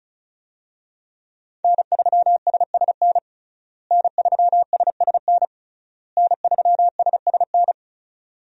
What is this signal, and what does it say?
Morse code, heard as a keyed tone, about 36 wpm, 700 Hz. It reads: N3SSN N3SSN N3SSN
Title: USA Call Signs - 5 Characters - Repeated 3x - 35wpm